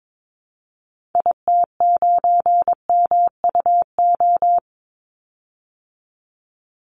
Morse code, heard as a keyed tone, about 22 wpm, 700 Hz. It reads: IT9MUO
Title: Call Signs Repeated 2x - 22wpm